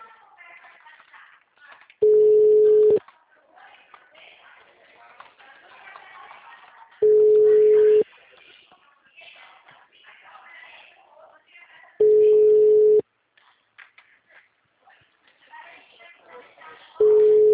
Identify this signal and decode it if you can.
Ringback tone (Europe, 425 Hz), with speech.
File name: out- -2017- - - .53174